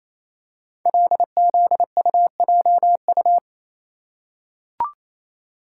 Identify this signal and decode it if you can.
Morse code, heard as a keyed tone, about 27 wpm, 690 Hz. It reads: LZUJU
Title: Code Groups - 5 Letters - 28wpm